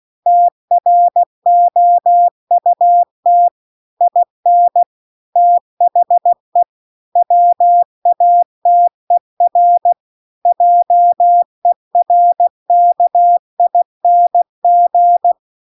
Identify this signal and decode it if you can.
Morse code, heard as a keyed tone, about 16 wpm, 700 Hz. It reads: TROUT IN THE WATER JERKING
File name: CW-The-Big-Twoheaded-River-2-16WPM